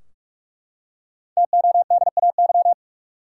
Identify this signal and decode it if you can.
Morse code, heard as a keyed tone, about 44 wpm, 700 Hz. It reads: TODAY
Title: Five Letter Words - 45wpm